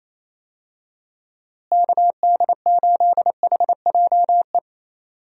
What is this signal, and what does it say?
Morse code, heard as a keyed tone, about 28 wpm, 700 Hz. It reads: KD8HJE
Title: USA Call Signs - 6 Characters - Repeated 2x - 28wpm